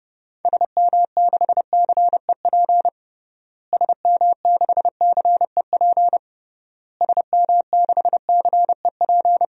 Morse code, heard as a keyed tone, about 32 wpm, 700 Hz. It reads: SM6CEP SM6CEP SM6CEP
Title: DXCC-100 Call Signs - 6 Characters - Repeated 3x - 30wpm